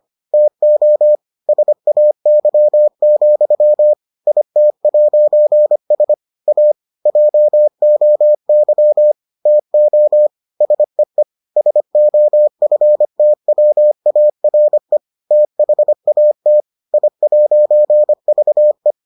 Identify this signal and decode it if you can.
Morse code, heard as a keyed tone, about 25 wpm, 600 Hz. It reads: TO SAY, IT'S A JOY TO SEE SOFTWARE THAT I'VE